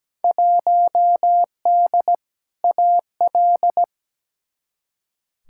Morse code, heard as a keyed tone, about 17 wpm, 700 Hz. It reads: 1D AL
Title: Field Day Exchanges - 17wpm